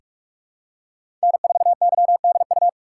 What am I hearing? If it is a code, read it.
Morse code, heard as a keyed tone, about 41 wpm, 700 Hz. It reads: N4YDU
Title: K1USN SST QSO Practice - 45wpm